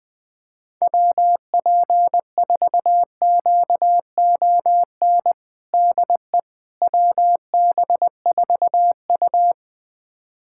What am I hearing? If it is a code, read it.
Morse code, heard as a keyed tone, about 20 wpm, 690 Hz. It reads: WP4QON DE WB4U
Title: USA Callsign DE Callsign 20wpm